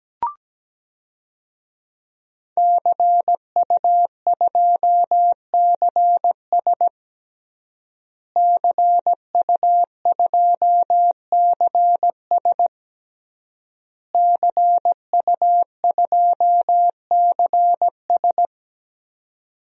Morse code, heard as a keyed tone, about 17 wpm, 700 Hz. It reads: CU2CS CU2CS CU2CS
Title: DXCC-100 Call Signs - 5 Characters - Repeated 3x - 17wpm